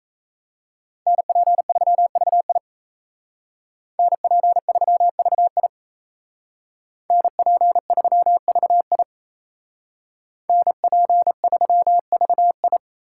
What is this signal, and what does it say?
Morse code, heard as a keyed tone, about 35 wpm, 700 Hz. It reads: NP3VI NP3VI NP3VI NP3VI